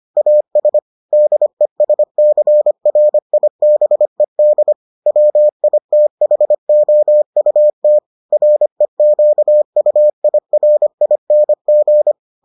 Morse code, heard as a keyed tone, about 25 wpm, 600 Hz. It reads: AS DESCRIBED WITHOUT REQUIRING